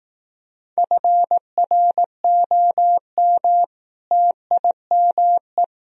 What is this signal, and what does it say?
Morse code, heard as a keyed tone, about 18 wpm, 700 Hz. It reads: FROM TIME
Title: Speed-Racing - Sets of 4 Words 15wpm